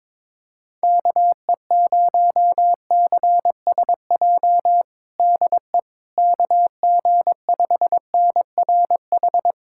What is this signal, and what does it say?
Morse code, heard as a keyed tone, about 22 wpm, 700 Hz. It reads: KE0CSJ DE KG5NRH
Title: USA Callsign DE Callsign - 6 Characters - 22wpm